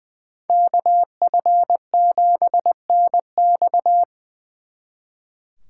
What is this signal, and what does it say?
Morse code, heard as a keyed tone, about 24 wpm, 700 Hz. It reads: KF7NX
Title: Speed Racing - USA Call Signs 20wpm